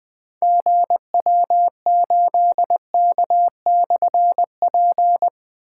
Morse code, heard as a keyed tone, about 20 wpm, 700 Hz. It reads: GW8K/P
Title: Difficult Call Signs 20wpm